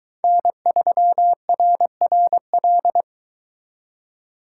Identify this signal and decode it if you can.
Morse code, heard as a keyed tone, about 22 wpm, 700 Hz. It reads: N3RRL